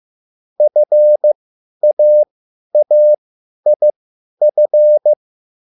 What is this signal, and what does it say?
Morse code, heard as a keyed tone, about 15 wpm, spaced as for 10 wpm, 600 Hz. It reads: FAAIF
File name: QSL / CW lesson